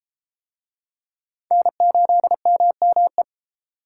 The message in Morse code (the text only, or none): N8MME